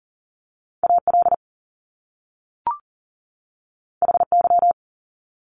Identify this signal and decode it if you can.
Morse code, heard as a keyed tone, about 40 wpm, 700 Hz. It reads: AL HY
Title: Top Two Letter Combinations in Words 40wpm